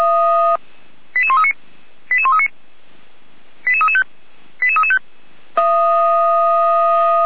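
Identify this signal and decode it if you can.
Five-tone selective call (ZVEI), heard as quick runs of five tones, tones of about 70 ms. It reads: 88139 88139 88335 88335
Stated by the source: ZVEI five-tone call received on an SDR, fire brigade Orlamünde (feuerwehr-orlamuende) recording archive.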